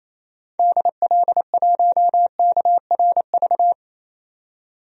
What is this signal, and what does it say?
Morse code, heard as a keyed tone, about 28 wpm, 700 Hz. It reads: DL1KRV
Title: DXCC-100 Call Signs - 6 Characters - 28wpm